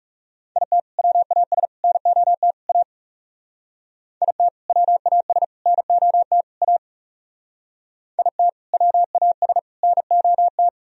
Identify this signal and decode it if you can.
Morse code, heard as a keyed tone, about 36 wpm, 700 Hz. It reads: IT WAS NOT A IT WAS NOT A IT WAS NOT
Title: Speed-Racing - Sets of 4 Words 30wpm